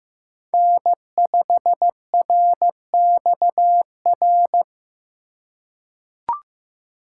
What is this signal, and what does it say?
Morse code, heard as a keyed tone, about 15 wpm, 700 Hz. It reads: N5RXR
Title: USA Call Signs - 5 Characters - 2x Character Spacing - 15wpm